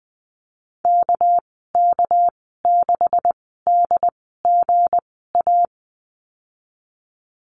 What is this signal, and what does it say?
Morse code, heard as a keyed tone, about 20 wpm, spaced as for 15 wpm, 700 Hz. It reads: KK6DGA
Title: 200 Lesson 32g - Letter k - Call Signs 20wpm